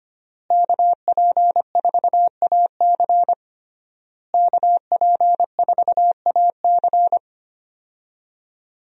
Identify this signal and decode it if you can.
Morse code, heard as a keyed tone, about 25 wpm, 700 Hz. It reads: KP4AC KP4AC
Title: USA Call Signs - 5 Characters - Repeated 3x - 25wpm